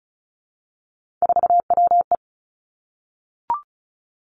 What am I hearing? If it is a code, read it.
Morse code, heard as a keyed tone, about 36 wpm, 700 Hz. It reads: 4WE
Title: Three Challenging Letter-Number Combinations 35wpm